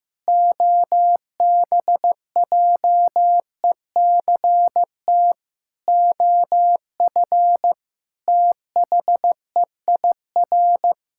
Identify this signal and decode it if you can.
Morse code, heard as a keyed tone, about 16 wpm, 700 Hz. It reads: OBJECT OF THEIR
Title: Sentences from Top 1000 Words - Review Part 5 - 15wpm